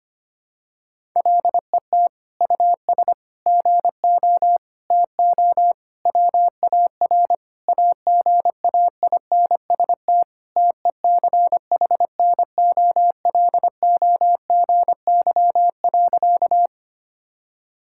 Morse code, heard as a keyed tone, about 25 wpm, 700 Hz. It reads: LET US GO TO WAR AGAINST TECHNOLOGY.
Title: Sentences from Top 1000 Words - Review Part 5 - 25wpm